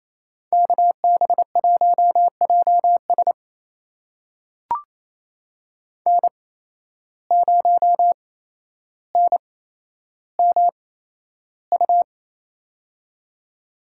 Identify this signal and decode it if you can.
Morse code, heard as a keyed tone, about 28 wpm, spaced as for 12 wpm, 700 Hz. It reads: KB1JS N0NMU
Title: USA Call Signs - 5 Characters - 8x Character Spacing - 28wpm